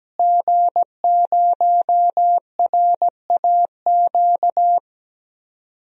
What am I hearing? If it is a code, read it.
Morse code, heard as a keyed tone, about 17 wpm, 700 Hz. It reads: G0RAQ
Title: DXCC-100 Call Signs - 5 Characters - Repeated 2x - 17wpm